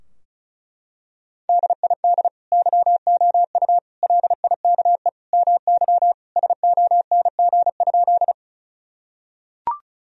Morse code, heard as a keyed tone, about 36 wpm, 700 Hz. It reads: DID YOU LIKE MY SONG?
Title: Sentences from Top 1000 Words - Review Part 2 - 35wpm